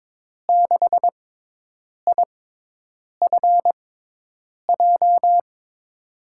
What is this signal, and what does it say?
Morse code, heard as a keyed tone, about 22 wpm, spaced as for 8 wpm, 700 Hz. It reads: 6IFJ
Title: USA Call Signs - 6x Character Spacing - 22wpm